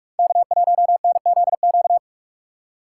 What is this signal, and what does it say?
Morse code, heard as a keyed tone, about 44 wpm, 700 Hz. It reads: K1NZQ
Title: USA Call Signs - 5 Characters - 6x Character Spacing - 45wpm